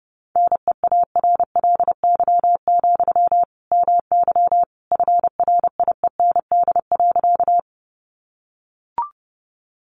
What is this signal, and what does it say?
Morse code, heard as a keyed tone, about 30 wpm, 700 Hz. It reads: NEARLY, MY FRIEND.
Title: Sentences from Top 1300 Words 30wpm